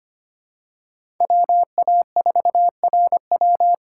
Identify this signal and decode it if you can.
Morse code, heard as a keyed tone, about 25 wpm, 700 Hz. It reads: WA4RW
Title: USA Call Signs - 5 Characters - 3x Character Spacing - 25wpm